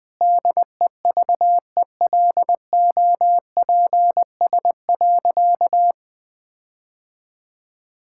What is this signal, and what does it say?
Morse code, heard as a keyed tone, about 20 wpm, 700 Hz. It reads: DEVELOPS.
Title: Sentences with 7-Word Phrases - 20wpm